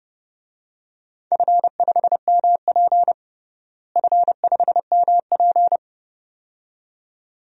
Morse code, heard as a keyed tone, about 30 wpm, 700 Hz. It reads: F5MP F5MP
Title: DXCC-100 Call Signs - 4 Characters - Repeated 2x - 30wpm